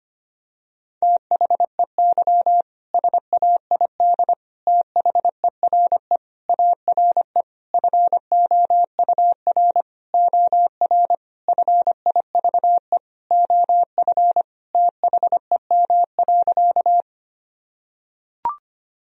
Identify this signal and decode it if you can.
Morse code, heard as a keyed tone, about 25 wpm, 700 Hz. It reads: THEY SAID THERE ARE FOUR OR FIVE OF THEM.
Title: Sentences from Top 1000 Words - Review Part 2 - 25wpm